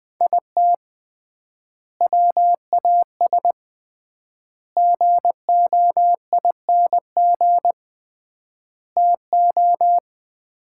Morse code, heard as a keyed tone, about 20 wpm, 700 Hz. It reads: IT WAS GOING TO
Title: 3x Word-Spacing - Sets of 5 Words - 20wpm